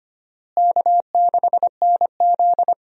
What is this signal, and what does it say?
Morse code, heard as a keyed tone, about 25 wpm, 700 Hz. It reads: K6NZ
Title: USA Call Signs - 4 Characters - Repeated 3x - 25wpm